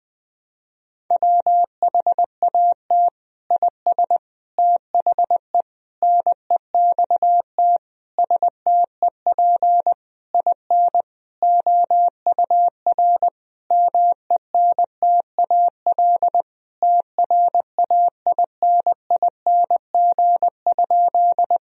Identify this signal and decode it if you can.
Morse code, heard as a keyed tone, about 21 wpm, 700 Hz. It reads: WHAT IS THE NEXT STEP IN OUR MENTAL TRAINING?
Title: Sentences from Top 1000 Words - Review Part 4 - 20wpm